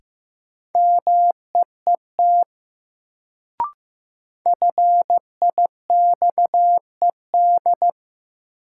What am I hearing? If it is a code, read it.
Morse code, heard as a keyed tone, about 15 wpm, 700 Hz. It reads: MEET FIXED
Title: Sentences from Top 1100 Words - New Word Drill Alt - 15wpm